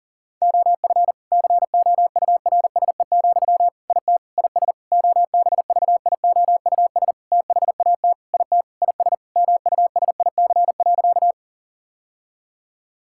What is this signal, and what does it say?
Morse code, heard as a keyed tone, about 40 wpm, 700 Hz. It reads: OF COURSE, IT IS OBVIOUS THAT IT IS MUSIC.
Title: Sentences from Top 1600 Words 40wpm